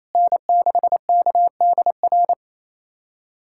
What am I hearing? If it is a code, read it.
Morse code, heard as a keyed tone, about 28 wpm, 700 Hz. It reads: N6KDR
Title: USA Callsign DE Callsign - 5 Characters - 28wpm